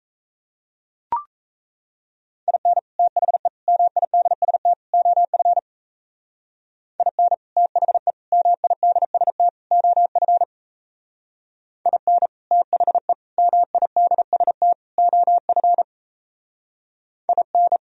Morse code, heard as a keyed tone, about 37 wpm, 700 Hz. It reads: IN THE MIDST OF IN THE MIDST OF IN THE MIDST OF IN